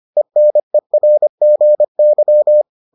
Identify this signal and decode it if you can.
Morse code, heard as a keyed tone, about 25 wpm, 600 Hz. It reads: ENERGY